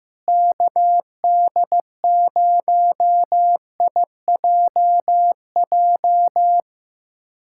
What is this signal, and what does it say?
Morse code, heard as a keyed tone, about 15 wpm, 700 Hz. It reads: KD0IJJ
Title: USA Callsign DE Callsign - 6 Characters - 15wpm